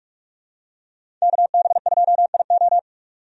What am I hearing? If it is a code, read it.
Morse code, heard as a keyed tone, about 46 wpm, 700 Hz. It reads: KB2IO
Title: USA Call Signs - 5 Characters - 45wpm